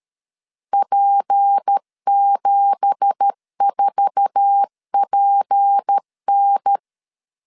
Morse code, heard as a keyed tone, about 13 wpm, 800 Hz. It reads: P74PN
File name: CW / DET / 1 9 1500C S C 12wpm 3811